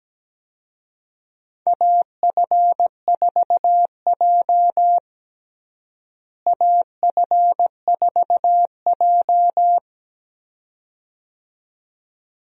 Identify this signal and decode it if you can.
Morse code, heard as a keyed tone, about 17 wpm, 700 Hz. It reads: AF4J AF4J
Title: USA Call Signs - 4 Characters - Repeated 2x - 17wpm